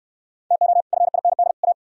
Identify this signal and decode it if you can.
Morse code, heard as a keyed tone, about 67 wpm, 700 Hz. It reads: TO FIND A